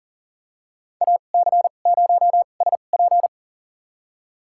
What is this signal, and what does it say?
Morse code, heard as a keyed tone, about 40 wpm, spaced as for 29 wpm, 700 Hz. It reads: AC0SP
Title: USA Call Signs - 2x Character Spacing - 40wpm